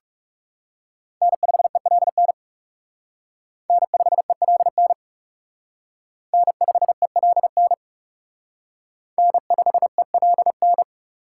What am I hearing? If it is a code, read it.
Morse code, heard as a keyed tone, about 37 wpm, 700 Hz. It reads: N5ELN N5ELN N5ELN N5ELN
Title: Speed Racing - USA 1x3 Call Signs 30wpm